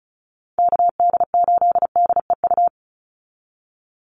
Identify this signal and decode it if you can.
Morse code, heard as a keyed tone, about 34 wpm, 700 Hz. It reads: KD8DEU
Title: NA SOTA Chaser Callsigns 35wpm